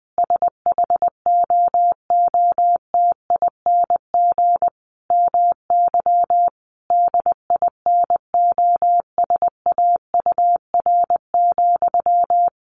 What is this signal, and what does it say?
Morse code, heard as a keyed tone, about 20 wpm, 700 Hz. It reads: SHOOTING MY DINOSAUR,
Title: A Sound Of Thunder 20wpm